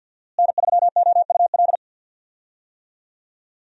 Morse code, heard as a keyed tone, about 52 wpm, 700 Hz. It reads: N3OUL
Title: USA Call Signs - 5 Characters - Repeated 3x - 50wpm